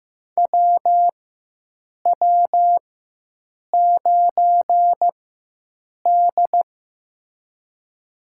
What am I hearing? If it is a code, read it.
Morse code, heard as a keyed tone, about 16 wpm, spaced as for 7 wpm, 700 Hz. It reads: WW9D
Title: USA Call Signs - 4x Character Spacing - 15wpm